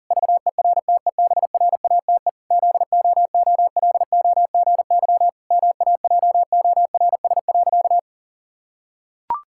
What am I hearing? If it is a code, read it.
Morse code, heard as a keyed tone, about 40 wpm, 700 Hz. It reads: VERTEBRATE ZOOLOGY MAJORS.
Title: Pangram Challenge v2 - 40wpm